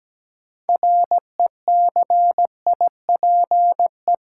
Morse code, heard as a keyed tone, about 17 wpm, 700 Hz. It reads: RECIPE
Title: Sentences from Top 1200 Words - New Word Drill Alt - 17wpm